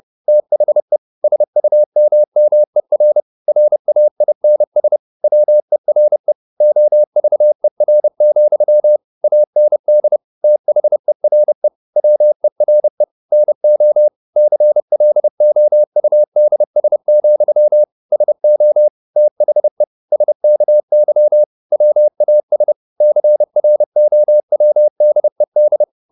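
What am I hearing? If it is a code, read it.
Morse code, heard as a keyed tone, about 30 wpm, 600 Hz. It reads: THE SUMMER RAINS WERE OVER, AND THERE WERE NO CLOUDS, SO THE SKY WAS CROWDED
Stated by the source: Morse code at 30 wpm, the novel Never by Ken Follett in English, from CW med Gnister.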